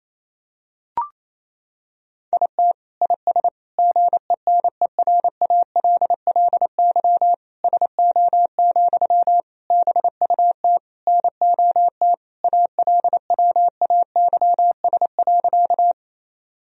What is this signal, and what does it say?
Morse code, heard as a keyed tone, about 28 wpm, 700 Hz. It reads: IT IS GENERALLY SO, BUT NOT ALWAYS.
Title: Sentences from Top 1000 Words - Review Part 4 - 28wpm